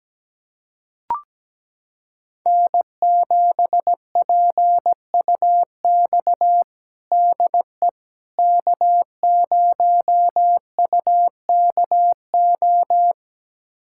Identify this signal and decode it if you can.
Morse code, heard as a keyed tone, about 17 wpm, 700 Hz. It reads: N7PUX DE K0UKO
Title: USA Callsign DE Callsign - 5 Characters - 17wpm